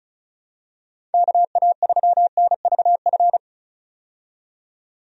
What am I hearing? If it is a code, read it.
Morse code, heard as a keyed tone, about 34 wpm, 700 Hz. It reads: KA3NVF
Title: USA Call Signs - 6 Characters - 35wpm